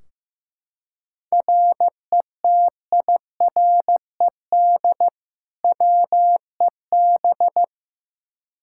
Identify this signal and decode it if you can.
Morse code, heard as a keyed tone, about 15 wpm, 700 Hz. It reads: RETIRED WEB
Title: K1USN SST QSO Practice 15wpm